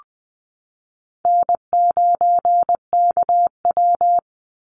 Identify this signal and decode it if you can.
Morse code, heard as a keyed tone, about 20 wpm, 700 Hz. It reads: N9KW
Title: NA SOTA Chaser Callsigns 20wpm